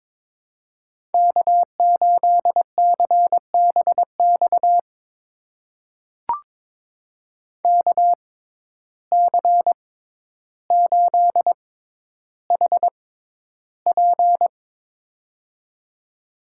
Morse code, heard as a keyed tone, about 22 wpm, spaced as for 11 wpm, 700 Hz. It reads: K8CBX KC8HP
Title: USA Call Signs - 5 Characters - 6x Character Spacing - 22wpm